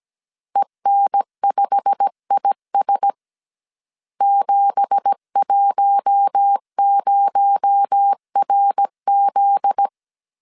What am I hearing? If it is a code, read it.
Morse code, heard as a keyed tone, about 17 wpm, 800 Hz. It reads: EN5IS 710RZ
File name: CW / DET / 1 9 1500C S C 16wpm 2909